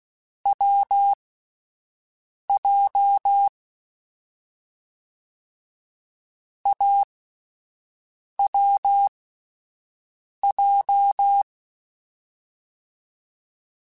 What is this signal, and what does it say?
Morse code, heard as a keyed tone, about 16 wpm, spaced as for 5 wpm, 800 Hz. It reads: WJ AWJ